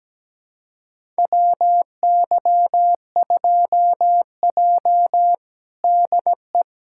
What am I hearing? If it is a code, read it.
Morse code, heard as a keyed tone, about 17 wpm, 700 Hz. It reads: WY2J DE
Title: USA Callsign DE Callsign - 4 Characters - 17wpm